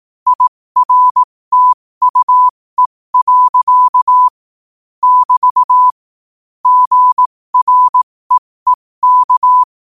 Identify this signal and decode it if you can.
Morse code, heard as a keyed tone, about 18 wpm, spaced as for 15 wpm, 1000 Hz. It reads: IRTUE. <BT> GREEK
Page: New quotes every day in morse code at 15 Words per minute.